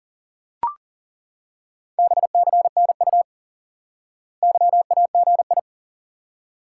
Morse code, heard as a keyed tone, about 40 wpm, 700 Hz. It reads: BCNU YAGI
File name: IQE Course 3 - Lesson 89 - YAGI - Two Elements - 40wpm